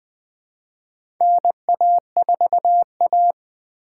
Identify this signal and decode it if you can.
Morse code, heard as a keyed tone, about 20 wpm, 700 Hz. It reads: NA4A
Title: 2023 POTA Activator Call Signs - 20wpm